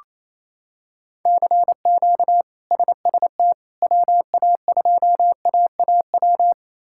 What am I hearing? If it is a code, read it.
Morse code, heard as a keyed tone, about 28 wpm, 700 Hz. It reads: CQ SST WA2AAW
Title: K1USN SST QSO Practice - 28wpm